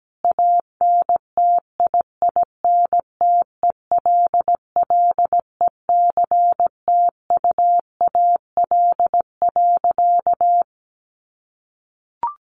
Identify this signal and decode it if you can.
Morse code, heard as a keyed tone, about 17 wpm, 700 Hz. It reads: ANTIINTELLECTUAL.
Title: A Sound Of Thunder 17wpm